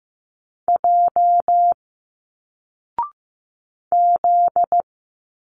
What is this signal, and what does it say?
Morse code, heard as a keyed tone, about 15 wpm, 700 Hz. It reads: J Z
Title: Single Letters 15wpm